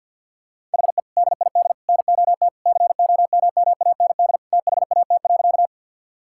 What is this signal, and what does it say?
Morse code, heard as a keyed tone, about 50 wpm, 700 Hz. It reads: HE DID NOT COMMAND THAT.
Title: Sentences from Top 1600 Words 50wpm